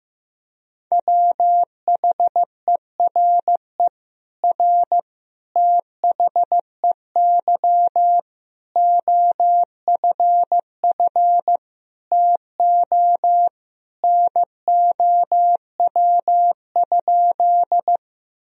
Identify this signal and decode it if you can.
Morse code, heard as a keyed tone, about 15 wpm, 700 Hz. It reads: WHERE R THEY OFF TO NOW?
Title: Sentences from Top 200 Words and Abbreviations 15wpm